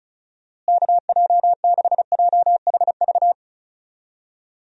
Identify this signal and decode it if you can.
Morse code, heard as a keyed tone, about 36 wpm, 700 Hz. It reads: KJ6JHV
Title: USA Call Signs - 6 Characters - Repeated 2x - 35wpm